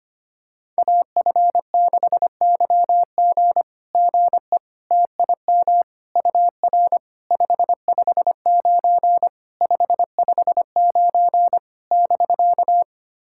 Morse code, heard as a keyed tone, about 25 wpm, 700 Hz. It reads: AF6YG GE TIM UR 559 559 <BK>
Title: SOTA QSO Practice 25wpm